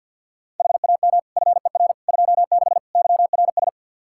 Morse code, heard as a keyed tone, about 47 wpm, 700 Hz. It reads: HAM FER 26 YRS
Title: Practice QSO - Exchange 3 - More Info - 50wpm